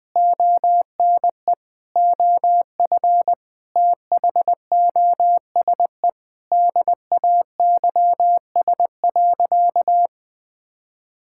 Morse code, heard as a keyed tone, about 20 wpm, 700 Hz. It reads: ONE OF THOSE DAYS.
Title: Sentences with 4-Word Phrases - Spoken Beginning - 20wpm